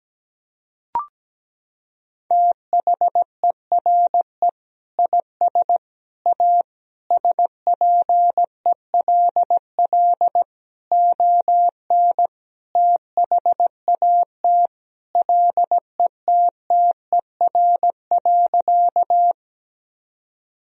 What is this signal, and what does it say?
Morse code, heard as a keyed tone, about 17 wpm, 700 Hz. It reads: THERE IS A SPELL ON THAT LETTER.
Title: Sentences from Top 1000 Words - Review Part 1 - 17wpm